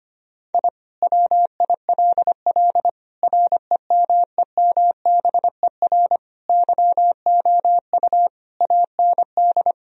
Morse code, heard as a keyed tone, about 25 wpm, 700 Hz. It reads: I WILL REMEMBER YOU AND
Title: Sentences from Top 1000 Words - Review Part 2 - 25wpm